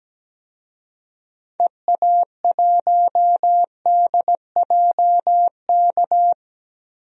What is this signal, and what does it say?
Morse code, heard as a keyed tone, about 17 wpm, 700 Hz. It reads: EA1DJK
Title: DXCC-100 Call Signs - 6 Characters - 17wpm